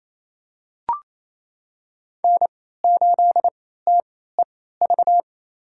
Morse code, heard as a keyed tone, about 28 wpm, spaced as for 16 wpm, 700 Hz. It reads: N8TEV
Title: USA Call Signs - 3x Character Spacing - 28wpm